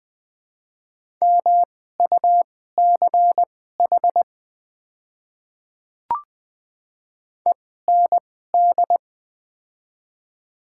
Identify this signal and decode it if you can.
Morse code, heard as a keyed tone, about 20 wpm, spaced as for 14 wpm, 700 Hz. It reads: MUCH END
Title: 2x Character Spacing - Top 300 Words - 20wpm